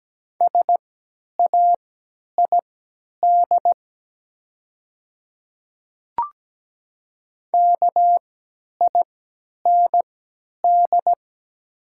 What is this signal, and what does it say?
Morse code, heard as a keyed tone, about 17 wpm, spaced as for 10 wpm, 700 Hz. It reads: SAID KIND
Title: 3x Character Spacing - Top 300 Words - 17wpm